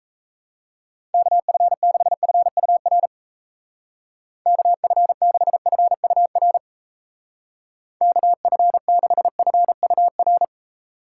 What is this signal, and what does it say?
Morse code, heard as a keyed tone, about 34 wpm, 700 Hz. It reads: KF6FUR KF6FUR KF6FUR
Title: Speed Racing - USA 2x3 Call Signs 28wpm